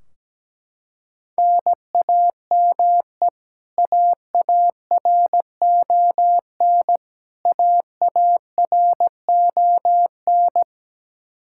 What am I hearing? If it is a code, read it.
Morse code, heard as a keyed tone, about 17 wpm, 700 Hz. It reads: NAME AARON AARON
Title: Practice QSO - Exchange 1 - Beginning - 17wpm